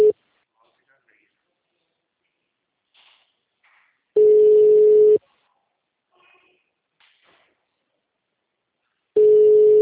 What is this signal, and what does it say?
Ringback tone (Europe, 425 Hz), with speech.
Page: Piracicaba - SP